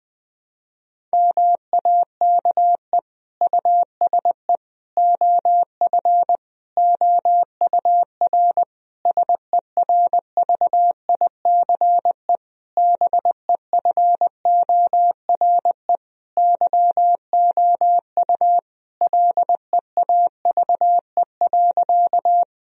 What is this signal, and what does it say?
Morse code, heard as a keyed tone, about 20 wpm, 700 Hz. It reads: MAKE USE OF OUR SERVICE BEFORE YOU LEAVE.
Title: Sentences from Top 1000 Words - Review Part 4 - 20wpm